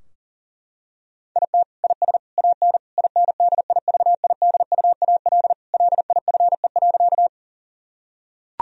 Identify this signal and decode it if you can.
Morse code, heard as a keyed tone, about 40 wpm, 700 Hz. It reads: IT IS AN INDIVIDUAL LIFE.
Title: Sentences from Top 1000 Words - Review Part 4 - 40wpm